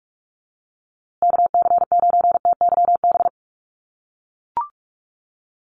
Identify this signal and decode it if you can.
Morse code, heard as a keyed tone, about 44 wpm, 700 Hz. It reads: KC9TYB
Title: USA Call Signs Repeated 3x 45wpm